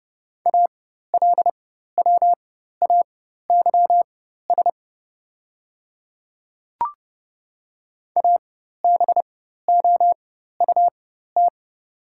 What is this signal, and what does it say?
Morse code, heard as a keyed tone, about 30 wpm, spaced as for 14 wpm, 700 Hz. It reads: ALWAYS ABOUT
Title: 4x Character Spacing - Top 300 Words - 30wpm